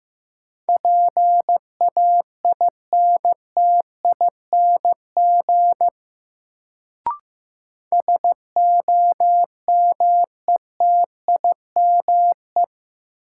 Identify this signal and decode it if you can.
Morse code, heard as a keyed tone, about 15 wpm, 700 Hz. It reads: PAINTING SOMETIME
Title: Sentences from Top 1000 Words - Review Part 6 - New Word Drill - 15wpm